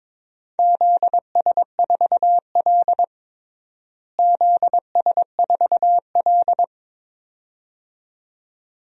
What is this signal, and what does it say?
Morse code, heard as a keyed tone, about 22 wpm, 700 Hz. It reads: ZS4L ZS4L
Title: DXCC-100 Call Signs - 4 Characters - Repeated 2x - 22wpm